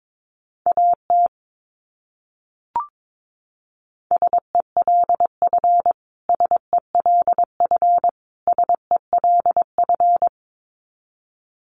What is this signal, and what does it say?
Morse code, heard as a keyed tone, about 22 wpm, 700 Hz. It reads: AT SELF SELF SELF
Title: Top 200 Words Repeated 22wpm